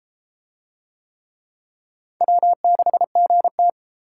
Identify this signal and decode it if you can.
Morse code, heard as a keyed tone, about 32 wpm, 700 Hz. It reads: W6GT